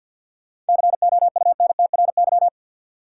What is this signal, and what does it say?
Morse code, heard as a keyed tone, about 49 wpm, 700 Hz. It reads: COUNTRY
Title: Sentences from Top 300 Words - New Word Drill - 50wpm